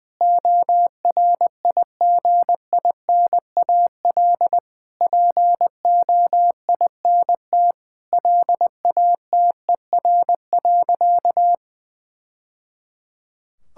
Morse code, heard as a keyed tone, about 20 wpm, 700 Hz. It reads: ORIGINAL POINT LATER.
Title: Sentences from Top 1000 Words - Review Part 6 - 20wpm